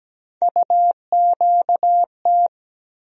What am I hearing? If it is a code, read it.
Morse code, heard as a keyed tone, about 17 wpm, 700 Hz. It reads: UQT
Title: USA Call Sign Postfixes - 3 Letters 17wpm